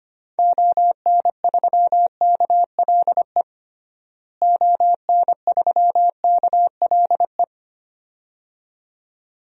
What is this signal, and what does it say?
Morse code, heard as a keyed tone, about 25 wpm, 700 Hz. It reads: ON3KLE ON3KLE
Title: DXCC-100 Call Signs - 6 Characters - Repeated 2x - 25wpm